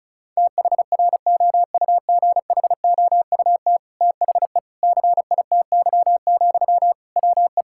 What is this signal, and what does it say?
Morse code, heard as a keyed tone, about 36 wpm, 700 Hz. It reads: THROUGHOUT THE CITY, WE
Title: Sentences from Top 1000 Words - Review Part 5 - 35wpm